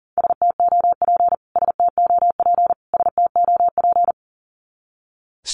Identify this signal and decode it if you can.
Morse code, heard as a keyed tone, about 40 wpm, 700 Hz. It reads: STOP STOP STOP
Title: Top 500 Words Repeated 40wpm